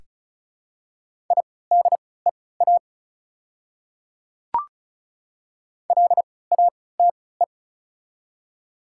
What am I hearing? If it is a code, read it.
Morse code, heard as a keyed tone, about 36 wpm, spaced as for 20 wpm, 700 Hz. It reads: IDEA LATE
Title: 3x Character Spacing - Top 300 Words - 35wpm